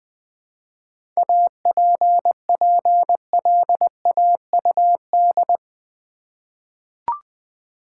Morse code, heard as a keyed tone, about 20 wpm, 700 Hz. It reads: APPLAUD
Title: 7 Letter Words - First 3 Letters Given - 20wpm